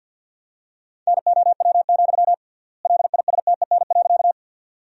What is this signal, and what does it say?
Morse code, heard as a keyed tone, about 50 wpm, 700 Hz. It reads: NOW, LISTEN.
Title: 3x Word-Spacing - Sentences from Top 400 Words - 50wpm